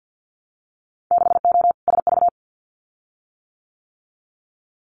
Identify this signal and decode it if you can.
Morse code, heard as a keyed tone, about 50 wpm, 700 Hz. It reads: 6O SV
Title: Winter Field Day Exchanges Encore 50wpm